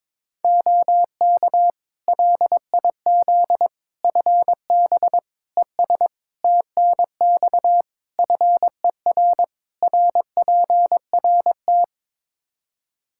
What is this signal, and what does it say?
Morse code, heard as a keyed tone, about 22 wpm, 700 Hz. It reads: OK LIZ FB ES TNX FER RPRT
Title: K1USN SST QSO Practice 22wpm